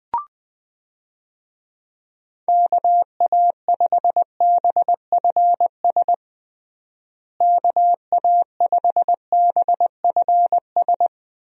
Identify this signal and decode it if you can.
Morse code, heard as a keyed tone, about 20 wpm, 700 Hz. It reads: KA5BFS KA5BFS
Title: USA Call Signs - 6 Characters - Repeated 2x - 20wpm